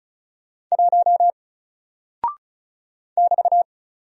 Morse code, heard as a keyed tone, about 36 wpm, 700 Hz. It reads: 1 <BT>
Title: Single Character (Letter:Number:Punctuation:Prosign) 35wpm